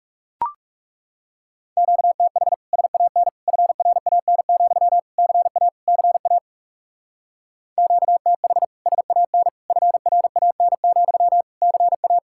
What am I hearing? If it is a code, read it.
Morse code, heard as a keyed tone, about 36 wpm, 700 Hz. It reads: QTH SAN FRAN, CA CA QTH SAN FRAN, CA
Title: Speed-Racing Common QSO Phrases Encore - 30wpm